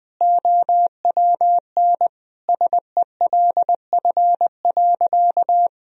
Morse code, heard as a keyed tone, about 20 wpm, 700 Hz. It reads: OWN SELF.
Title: Sentences from Top 1000 Words - Review Part 1 - 20wpm